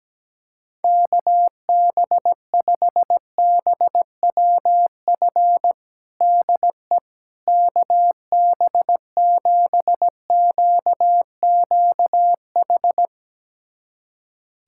Morse code, heard as a keyed tone, about 17 wpm, 700 Hz. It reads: KB5BWF DE KB7QQH
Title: USA Callsign DE Callsign 17wpm